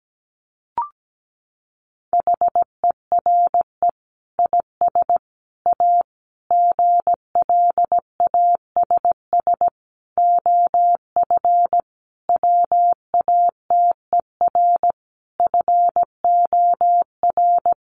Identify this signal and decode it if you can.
Morse code, heard as a keyed tone, about 17 wpm, 700 Hz. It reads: HERE IS A GLASS OF WATER FOR
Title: Sentences from Top 1300 Words 17wpm